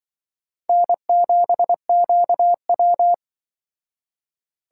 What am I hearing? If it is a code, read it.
Morse code, heard as a keyed tone, about 24 wpm, 700 Hz. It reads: N7QW
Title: Speed Racing - USA 1x2 Call Signs 20wpm